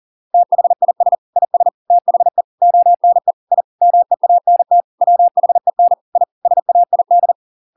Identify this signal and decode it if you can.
Morse code, heard as a keyed tone, about 40 wpm, 700 Hz. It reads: THIS IS THE ONE I MEANT WHEN I SAID